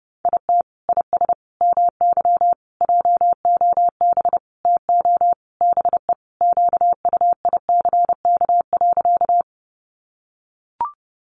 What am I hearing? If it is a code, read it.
Morse code, heard as a keyed tone, about 28 wpm, 700 Hz. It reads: IT IS MY JOB TO BE QUICK.
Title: Sentences from Top 600 Words 30wpm